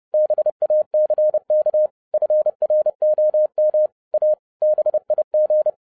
Morse code, heard as a keyed tone, about 30 wpm, 610 Hz. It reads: BACK FROM A BIG
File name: 2011-07-mp3 / n55 BACK FROM A BIG ACCIDENT 30wpm